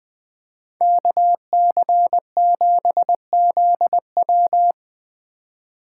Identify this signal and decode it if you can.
Morse code, heard as a keyed tone, about 20 wpm, 700 Hz. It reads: KC7ZW
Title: Speed Racing - USA 2x2 Call Signs 17wpm